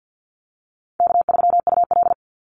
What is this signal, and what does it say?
Morse code, heard as a keyed tone, about 50 wpm, 700 Hz. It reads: K3UL
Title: Call Signs Repeated 2x - 50wpm